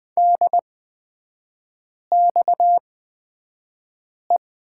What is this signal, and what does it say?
Morse code, heard as a keyed tone, about 20 wpm, spaced as for 5 wpm, 700 Hz. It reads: DXE
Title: 243 Lesson 39e - Letter z - Three Characters 20wpm